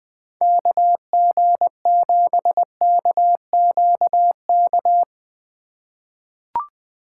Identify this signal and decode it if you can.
Morse code, heard as a keyed tone, about 20 wpm, 700 Hz. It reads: KG7KQK
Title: USA Call Signs - 6 Characters - 4x Character Spacing - 20wpm